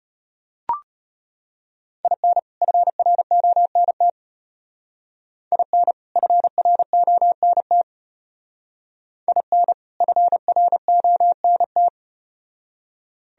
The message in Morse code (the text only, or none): IN FRONT IN FRONT IN FRONT